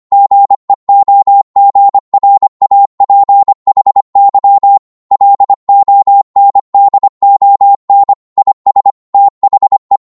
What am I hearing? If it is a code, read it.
Morse code, heard as a keyed tone, about 25 wpm, 800 Hz. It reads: GEOGRAPHY LONDON IS THE